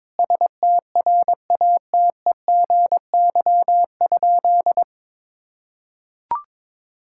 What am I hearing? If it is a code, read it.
Morse code, heard as a keyed tone, about 22 wpm, 700 Hz. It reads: STRATEGY?
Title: Sentences from Top 1000 Words - Review Part 5 - 22wpm